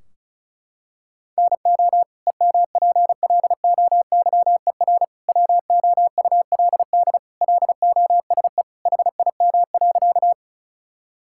Morse code, heard as a keyed tone, about 36 wpm, 700 Hz. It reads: NO EMPLOYER WOULD LOSE HIM.
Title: Sentences from Top 1600 Words 35wpm